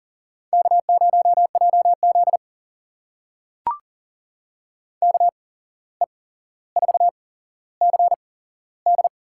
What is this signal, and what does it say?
Morse code, heard as a keyed tone, about 40 wpm, spaced as for 16 wpm, 700 Hz. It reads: K0JZ KE4CD
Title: USA Call Signs - 8x Character Spacing - 40wpm